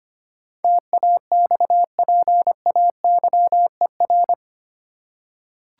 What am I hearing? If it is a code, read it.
Morse code, heard as a keyed tone, about 25 wpm, 700 Hz. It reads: TAXPAYER
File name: 8 Letter Words - First 3 Letters Given - 25wpm